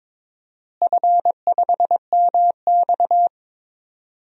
Morse code, heard as a keyed tone, about 22 wpm, 700 Hz. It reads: F5MX
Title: DXCC-100 Call Signs - 4 Characters - Repeated 3x - 22wpm